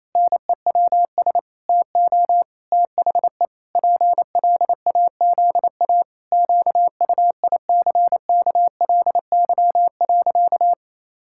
Morse code, heard as a keyed tone, about 28 wpm, 700 Hz. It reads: NEWS TO THE PLAZA QUICKLY.
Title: Pangram Challenge v2 - 28wpm